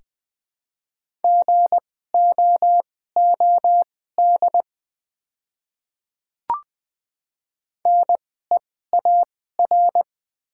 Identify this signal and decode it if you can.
Morse code, heard as a keyed tone, about 21 wpm, spaced as for 15 wpm, 700 Hz. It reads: GOOD NEAR
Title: 2x Character Spacing - Top 200 Words - 20wpm